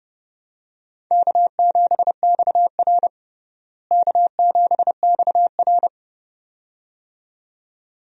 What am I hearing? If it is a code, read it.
Morse code, heard as a keyed tone, about 30 wpm, 700 Hz. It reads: K7XR K7XR
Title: USA Call Signs - 4 Characters - Repeated 2x - 30wpm